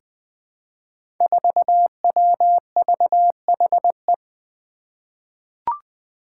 Morse code, heard as a keyed tone, about 20 wpm, 700 Hz. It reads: 4WVHE
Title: Five Challenging Letter-Number Combinations 20wpm